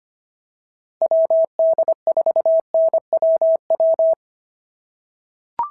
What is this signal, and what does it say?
Morse code, heard as a keyed tone, about 25 wpm, 690 Hz. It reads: WD4NWW
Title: SOTA QSO Practice 25wpm